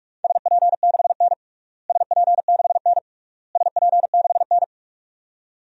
Morse code, heard as a keyed tone, about 46 wpm, 700 Hz. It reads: SP6N SP6N SP6N
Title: DXCC-100 Call Signs - 4 Characters - Repeated 3x - 45wpm